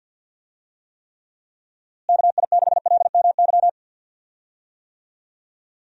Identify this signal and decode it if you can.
Morse code, heard as a keyed tone, about 48 wpm, 700 Hz. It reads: KI6LMY
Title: USA Call Signs - 6 Characters - Repeated 3x - 50wpm